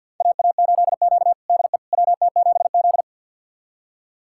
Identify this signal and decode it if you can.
Morse code, heard as a keyed tone, about 45 wpm, 700 Hz. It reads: AA8Q DE WT7Z
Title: USA Callsign DE Callsign - 4 Characters - 50wpm